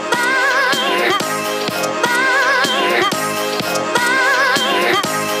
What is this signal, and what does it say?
Emergency Alert System (SAME) tones (header bursts), heard as harsh buzzing bursts. It is heard from 0.2 to 4.5 s.